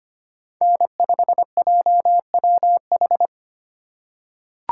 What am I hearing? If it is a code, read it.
Morse code, heard as a keyed tone, about 25 wpm, 700 Hz. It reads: N5JWH
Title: USA Call Signs - 5 Characters - 2x Character Spacing - 25wpm